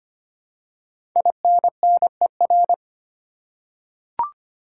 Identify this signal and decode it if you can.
Morse code, heard as a keyed tone, about 25 wpm, 700 Hz. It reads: INNER
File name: 5 Letter Words - First 3 Letters Given - 25wpm